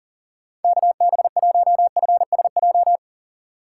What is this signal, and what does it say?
Morse code, heard as a keyed tone, about 40 wpm, 700 Hz. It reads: KB1FSJ
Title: USA Call Signs - 6 Characters - 6x Character Spacing - 40wpm